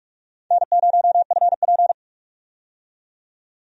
Morse code, heard as a keyed tone, about 46 wpm, 700 Hz. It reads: N0FP
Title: USA Call Signs - 4 Characters - 45wpm